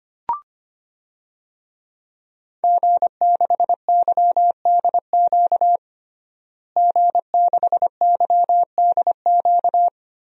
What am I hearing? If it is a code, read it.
Morse code, heard as a keyed tone, about 25 wpm, 700 Hz. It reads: G6YDQ G6YDQ
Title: DXCC-100 Call Signs - 5 Characters - Repeated 2x - 25wpm